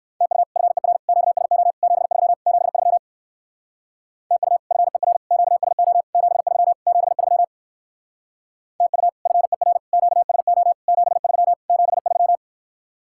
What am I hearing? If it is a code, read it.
Morse code, heard as a keyed tone, about 59 wpm, 700 Hz. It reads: TU FER QSO 73 73 TU FER QSO 73 73 TU FER QSO 73 73